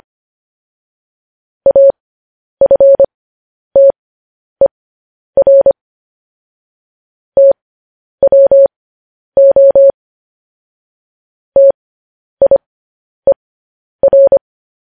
Morse code, heard as a keyed tone, about 25 wpm, spaced as for 10 wpm, 550 Hz. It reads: AFTER TWO TIER
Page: news-headlines-in-morse-code-at-10-wpm